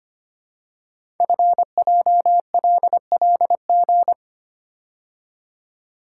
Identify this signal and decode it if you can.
Morse code, heard as a keyed tone, about 25 wpm, 700 Hz. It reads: FJLLG
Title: Code Groups - 5 Letters - 25wpm